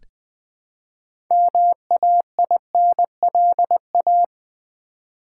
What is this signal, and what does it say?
Morse code, heard as a keyed tone, about 20 wpm, 700 Hz. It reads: MAINLA
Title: Build A Word - 2 Letters At A Time - 8-Letter Words - 20wpm